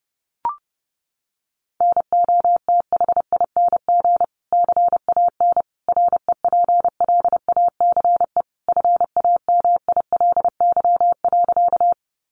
Morse code, heard as a keyed tone, about 30 wpm, 700 Hz. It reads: NOTHING CAN REPLACE FAMILY.
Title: Sentences from Top 1400 Words 30wpm